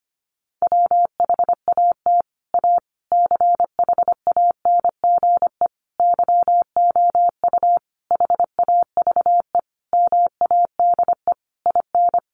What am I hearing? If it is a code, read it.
Morse code, heard as a keyed tone, about 25 wpm, 700 Hz. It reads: WHAT A CHANGE YOU HAVE MADE IN
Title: Sentences from Top 600 Words 25wpm